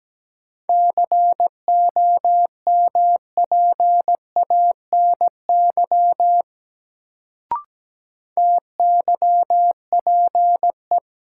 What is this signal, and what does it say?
Morse code, heard as a keyed tone, about 17 wpm, 700 Hz. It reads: COMPANY TYPE
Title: Sentences from Top 600 Words - New Word Drill - 17wpm